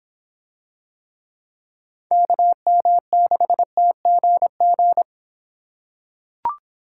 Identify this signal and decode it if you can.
Morse code, heard as a keyed tone, about 24 wpm, 700 Hz. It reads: KM6TGG